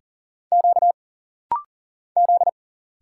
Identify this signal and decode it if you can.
Morse code, heard as a keyed tone, about 39 wpm, 700 Hz. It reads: Q Z
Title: Single Letters - Rapid-Fire 40wpm